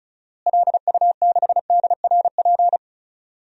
Morse code, heard as a keyed tone, about 36 wpm, 700 Hz. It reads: LU6DRP
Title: DXCC-100 Call Signs - 6 Characters - Repeated 3x - 35wpm